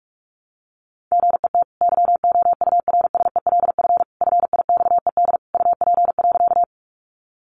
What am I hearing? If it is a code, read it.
Morse code, heard as a keyed tone, about 44 wpm, 700 Hz. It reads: GET YOURSELF FIXED UP.
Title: Sentences from Top 1100 Words 45wpm